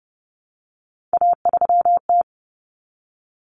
Morse code, heard as a keyed tone, about 30 wpm, 700 Hz. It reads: A3T
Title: Common Cut Numbers - 3-Digit 30wpm